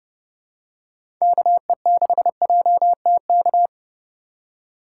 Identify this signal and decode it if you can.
Morse code, heard as a keyed tone, about 30 wpm, 700 Hz. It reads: KE6JTK
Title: USA Call Signs - 4x Character Spacing - 30wpm